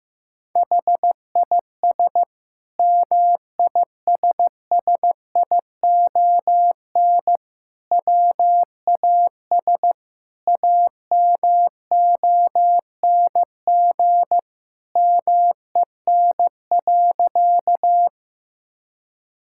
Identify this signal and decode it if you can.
Morse code, heard as a keyed tone, about 15 wpm, 700 Hz. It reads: HIS MISSION WAS AMONG MEN.
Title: Sentences from Top 1600 Words 15wpm